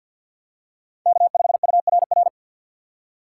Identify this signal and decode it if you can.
Morse code, heard as a keyed tone, about 50 wpm, 700 Hz. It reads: K5URR
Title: USA Call Signs - 5 Characters - 6x Character Spacing - 50wpm